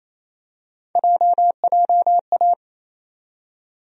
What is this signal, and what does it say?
Morse code, heard as a keyed tone, about 28 wpm, 700 Hz. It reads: JJA
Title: USA Call Sign Postfixes - 3 Letters 28wpm